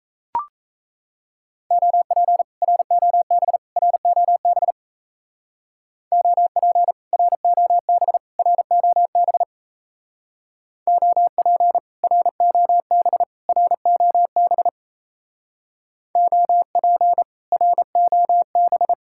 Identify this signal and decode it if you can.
Morse code, heard as a keyed tone, about 33 wpm, 700 Hz. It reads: OP ROB ROB OP ROB ROB OP ROB ROB OP ROB
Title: Speed-Racing Common QSO Phrases Encore - 28wpm